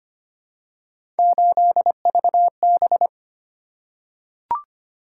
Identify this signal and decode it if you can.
Morse code, heard as a keyed tone, about 25 wpm, 700 Hz. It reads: 8VB
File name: Three Letter-Number Combinations - 25wpm